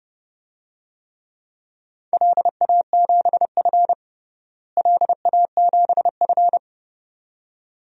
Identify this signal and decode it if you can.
Morse code, heard as a keyed tone, about 32 wpm, 700 Hz. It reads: LA7F LA7F
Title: DXCC-100 Call Signs - 4 Characters - Repeated 2x - 30wpm